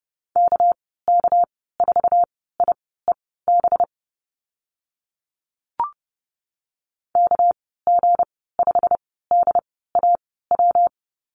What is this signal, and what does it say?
Morse code, heard as a keyed tone, about 30 wpm, spaced as for 17 wpm, 700 Hz. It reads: KK4IEB KG5DAW
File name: Lesson 37g - Letter x - Call Signs 30wpm